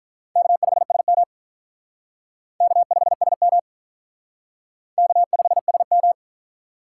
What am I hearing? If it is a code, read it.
Morse code, heard as a keyed tone, about 44 wpm, 700 Hz. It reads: K5SM K5SM K5SM